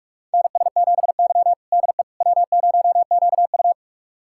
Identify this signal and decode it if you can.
Morse code, heard as a keyed tone, about 46 wpm, 700 Hz. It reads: NS7Y DE W0QU
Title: USA Callsign DE Callsign - 4 Characters - 45wpm